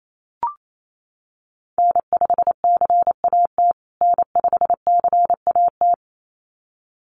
Morse code, heard as a keyed tone, about 28 wpm, 700 Hz. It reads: N5CAT N5CAT
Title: USA Call Signs Repeated 2x 28wpm